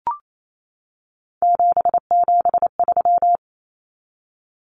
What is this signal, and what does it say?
Morse code, heard as a keyed tone, about 28 wpm, 700 Hz. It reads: 773
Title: Common Cut Numbers - 3-Digit 28wpm